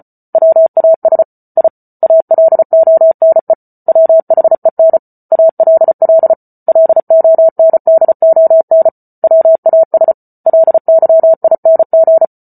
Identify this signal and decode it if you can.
Morse code, heard as a keyed tone, about 34 wpm, 650 Hz. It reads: WAS I ALONE WHEN ALL LONDON WAS LYING